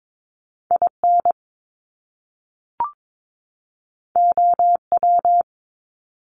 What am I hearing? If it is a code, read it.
Morse code, heard as a keyed tone, about 22 wpm, 700 Hz. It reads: IN OW
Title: Top Two Letter Combinations in Words 22wpm